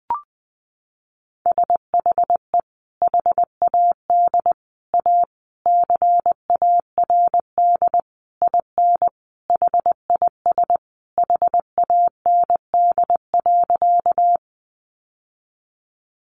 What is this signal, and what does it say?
Morse code, heard as a keyed tone, about 20 wpm, 700 Hz. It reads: SHE HAD A CARD IN HIS HAND.
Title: Sentences from Top 600 Words 20wpm